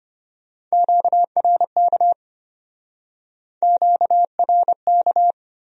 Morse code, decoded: QRK QRK

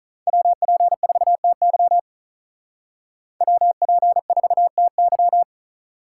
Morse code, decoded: WP4TY WP4TY

